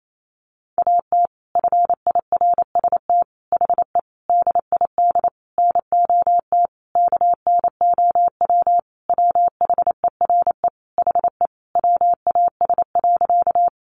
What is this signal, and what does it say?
Morse code, heard as a keyed tone, about 28 wpm, 700 Hz. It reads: AT FIRST HE DID NOT KNOW WHERE HE WAS.
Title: Sentences from Top 600 Words 28wpm